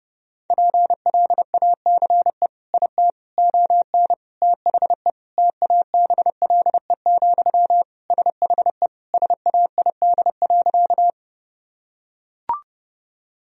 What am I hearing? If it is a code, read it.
Morse code, heard as a keyed tone, about 30 wpm, 700 Hz. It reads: PLACE IT ON THE TABLE, SHE SAID.
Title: Sentences from Top 1000 Words - Review Part 2 - 30wpm